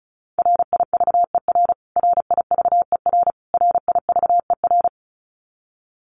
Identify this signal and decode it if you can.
Morse code, heard as a keyed tone, about 34 wpm, 700 Hz. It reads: RIVER RIVER RIVER
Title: Top 500 Words Repeated 35wpm